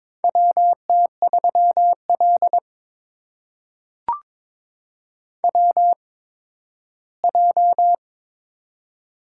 Morse code, heard as a keyed tone, about 22 wpm, spaced as for 13 wpm, 700 Hz. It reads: WT3L WJ
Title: USA Call Signs - 4 Characters - 8x Character Spacing - 22wpm